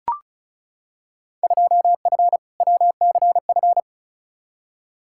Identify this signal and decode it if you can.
Morse code, heard as a keyed tone, about 34 wpm, 700 Hz. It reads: 2F WCF
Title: Field Day Exchanges Encore - 35wpm